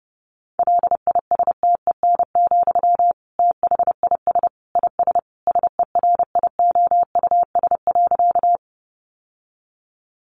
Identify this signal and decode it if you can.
Morse code, heard as a keyed tone, about 30 wpm, 700 Hz. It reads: LISTEN, THIS IS SERIOUS.
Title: Sentences from Top 1100 Words 30wpm